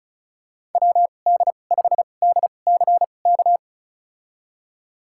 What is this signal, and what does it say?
Morse code, heard as a keyed tone, about 34 wpm, spaced as for 25 wpm, 700 Hz. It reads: WD5DCK
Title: USA Call Signs - 6 Characters - 2x Character Spacing - 35wpm